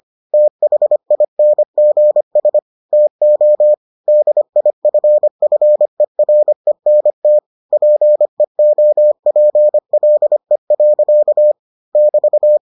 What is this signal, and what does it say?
Morse code, heard as a keyed tone, about 25 wpm, 600 Hz. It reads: THINGS TO DIFFERENT PEOPLE. <BT>